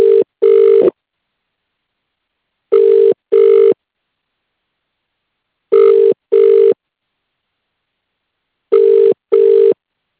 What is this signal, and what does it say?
Ringback tone (UK), 400 + 450 Hz.